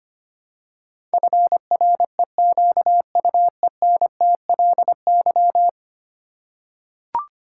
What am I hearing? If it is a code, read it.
Morse code, heard as a keyed tone, about 25 wpm, 700 Hz. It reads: FREQUENTLY
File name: 10 Letter Words - First 3 Letters Given - 25wpm